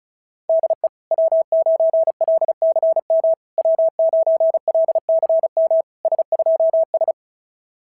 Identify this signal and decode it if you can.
Morse code, heard as a keyed tone, about 36 wpm, 690 Hz. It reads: DE W9LCM W9LCM S2S
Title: SOTA QSO Practice 35wpm